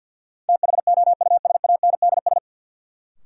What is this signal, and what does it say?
Morse code, heard as a keyed tone, about 51 wpm, 700 Hz. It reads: THOUSANDS
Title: Sets of 4 Words - New Word Drill - 50wpm